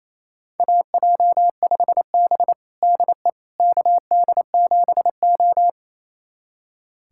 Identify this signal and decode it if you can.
Morse code, heard as a keyed tone, about 28 wpm, 700 Hz. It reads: AJ5B DE KD7O
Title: USA Callsign DE Callsign - 4 Characters - 28wpm